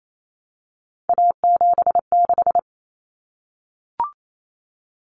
Morse code, heard as a keyed tone, about 28 wpm, 700 Hz. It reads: A76
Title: Common Cut Numbers - 3-Digit 28wpm